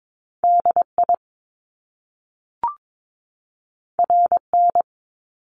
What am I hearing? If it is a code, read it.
Morse code, heard as a keyed tone, about 22 wpm, 700 Hz. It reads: DI RN
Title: Top Two Letter Combinations in Words 22wpm